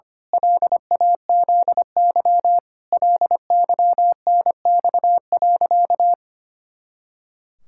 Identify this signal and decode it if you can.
Morse code, heard as a keyed tone, about 25 wpm, 700 Hz. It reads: LAZY LYNX.
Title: Pangram Challenge v2 - 25wpm